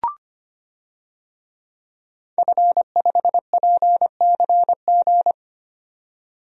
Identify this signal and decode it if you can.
Morse code, heard as a keyed tone, about 25 wpm, 700 Hz. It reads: F5PCG
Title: DXCC-100 Call Signs - 5 Characters - 25wpm